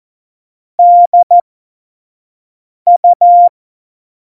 Morse code, heard as a keyed tone, about 14 wpm, spaced as for 5 wpm, 700 Hz. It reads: DU
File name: Morsekurs / Witze / Witz 14 April 2020, 25 F70